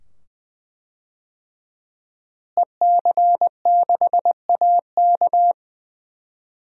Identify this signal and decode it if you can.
Morse code, heard as a keyed tone, about 20 wpm, 700 Hz. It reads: EC6AK
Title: DXCC-100 Call Signs - 5 Characters - Repeated 2x - 20wpm